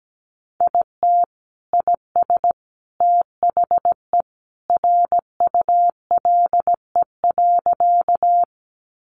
Morse code, heard as a keyed tone, about 17 wpm, 700 Hz. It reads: IT IS THE RULE.
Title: Sentences from Top 400 Words 17wpm